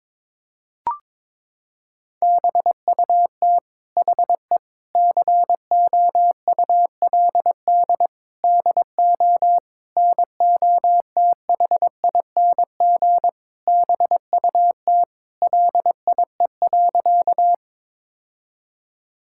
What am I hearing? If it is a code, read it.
Morse code, heard as a keyed tone, about 22 wpm, 700 Hz. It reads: BUT HE COULD DO NOTHING BUT LIE.
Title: Sentences from Top 1600 Words 22wpm